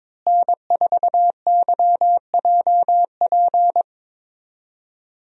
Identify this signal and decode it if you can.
Morse code, heard as a keyed tone, about 21 wpm, 700 Hz. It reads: N4YJP
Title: USA Call Signs - 5 Characters - Repeated 3x - 22wpm